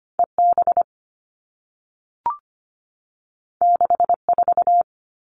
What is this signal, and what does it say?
Morse code, heard as a keyed tone, about 25 wpm, 700 Hz. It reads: EB 64
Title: All Cut Numbers - 2-Digit 25wpm